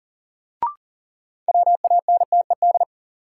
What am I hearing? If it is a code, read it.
Morse code, heard as a keyed tone, about 37 wpm, 700 Hz. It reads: WANTED
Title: Sets of 2 Words - New Word Drill - 40wpm